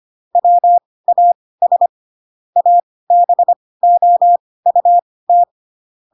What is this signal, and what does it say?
Morse code, heard as a keyed tone, about 25 wpm, spaced as for 18 wpm, 700 Hz. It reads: WAS ABOUT
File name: Pinocchio ch3 12 25 ik1hge